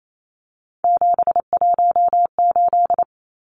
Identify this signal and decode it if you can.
Morse code, heard as a keyed tone, about 28 wpm, 700 Hz. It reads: 718
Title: Numbers - 3-Digit 28wpm